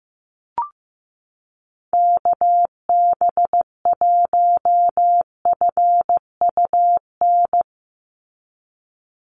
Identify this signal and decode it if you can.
Morse code, heard as a keyed tone, about 15 wpm, 700 Hz. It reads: KB1FUN
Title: USA Call Signs 15wpm